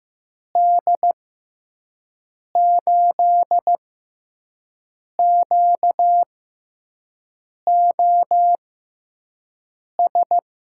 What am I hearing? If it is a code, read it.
Morse code, heard as a keyed tone, about 15 wpm, spaced as for 5 wpm, 700 Hz. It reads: D8QOS
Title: USA Call Signs - 6 Characters - 6x Character Spacing - 15wpm